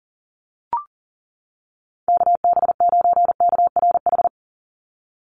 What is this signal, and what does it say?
Morse code, heard as a keyed tone, about 37 wpm, 700 Hz. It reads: KB9KRH
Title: USA Call Signs 40wpm